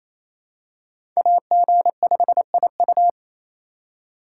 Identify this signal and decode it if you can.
Morse code, heard as a keyed tone, about 28 wpm, 700 Hz. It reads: AG5IU
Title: USA Call Signs - 5 Characters - 4x Character Spacing - 28wpm